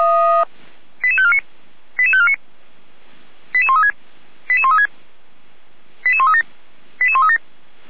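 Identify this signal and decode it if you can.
Five-tone selective call (ZVEI), heard as quick runs of five tones, tones of about 70 ms. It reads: 88549 88549 88136 88136 88137 88137
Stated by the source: ZVEI five-tone selective call, address 88190, received on an SDR.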